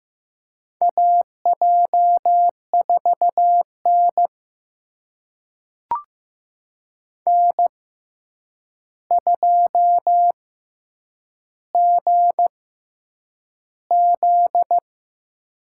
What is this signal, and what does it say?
Morse code, heard as a keyed tone, about 15 wpm, spaced as for 8 wpm, 700 Hz. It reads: AJ4N N2GZ
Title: USA Call Signs - 6x Character Spacing - 15wpm